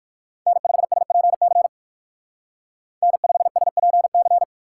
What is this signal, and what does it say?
Morse code, heard as a keyed tone, about 56 wpm, 700 Hz. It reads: N5SPC N5SPC
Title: Speed Racing - USA 1x3 Call Signs 45wpm